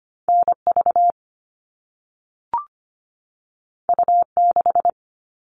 Morse code, heard as a keyed tone, about 25 wpm, 700 Hz. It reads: NV U6
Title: All Cut Numbers - 2-Digit 25wpm